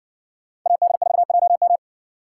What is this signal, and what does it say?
Morse code, heard as a keyed tone, about 60 wpm, 700 Hz. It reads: AD4JM